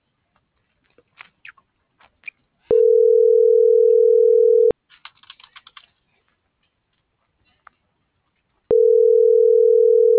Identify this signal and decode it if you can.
Ringback tone (North America), 440 + 480 Hz.